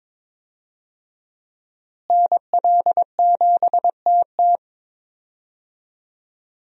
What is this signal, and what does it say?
Morse code, heard as a keyed tone, about 22 wpm, 700 Hz. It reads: NL7TT
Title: USA Call Signs - 5 Characters - Repeated 2x - 22wpm